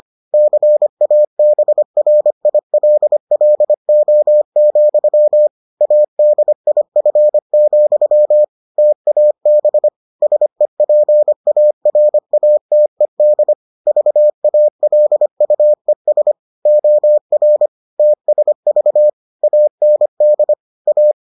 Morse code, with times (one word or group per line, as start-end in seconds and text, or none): 0.3-5.5 CABRILLO,
5.8-8.4 ADIF,
8.8-9.9 TAB
10.2-13.5 SEPARATED
13.9-16.3 VALUES
16.7-17.7 OR
18.0-19.1 TSV
19.4-20.5 AND
20.9-21.1 A